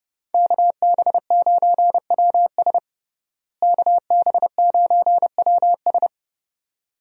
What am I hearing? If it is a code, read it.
Morse code, heard as a keyed tone, about 30 wpm, 700 Hz. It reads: KB9WS KB9WS
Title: USA Call Signs - 5 Characters - Repeated 2x - 30wpm